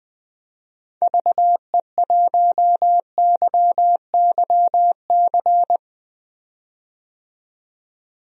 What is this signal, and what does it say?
Morse code, heard as a keyed tone, about 20 wpm, 700 Hz. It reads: VE1YYC
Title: DXCC-100 Call Signs - 6 Characters - 20wpm